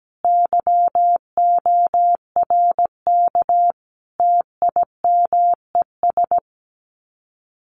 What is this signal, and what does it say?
Morse code, heard as a keyed tone, about 17 wpm, 700 Hz. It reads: YORK TIMES
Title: Sets of 5 Words Encore2 17wpm